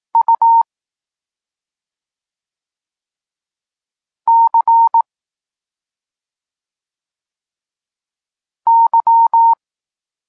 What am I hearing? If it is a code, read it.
Morse code, heard as a keyed tone, about 18 wpm, 920 Hz. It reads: U C Y
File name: Tekst-34 4